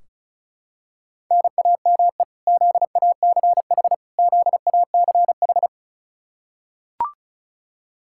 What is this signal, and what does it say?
Morse code, heard as a keyed tone, about 36 wpm, 700 Hz. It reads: NAME ZACH ZACH
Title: Practice QSO - Exchange 1 - Beginning - 35wpm